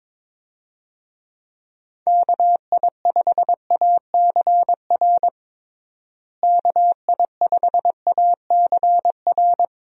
Morse code, heard as a keyed tone, about 22 wpm, 700 Hz. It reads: KI5ACR KI5ACR
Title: USA Call Signs - 6 Characters - Repeated 2x - 22wpm